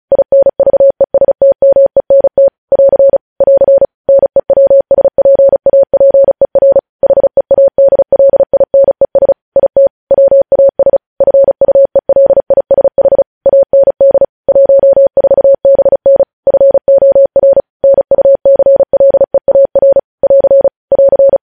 Morse code, heard as a keyed tone, about 36 wpm, 550 Hz. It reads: INVESTMENT <AR> <AR> NEWSPAPER HEADLINES IT WAS FUELISH AND 14BN FOR NUCLEAR <AR> <AR>
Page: news headlines in morse code at 35 wpm